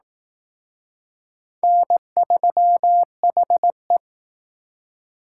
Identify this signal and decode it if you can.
Morse code, heard as a keyed tone, about 19 wpm, 700 Hz. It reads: N3HE
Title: Speed Racing - USA 1x2 Call Signs 15wpm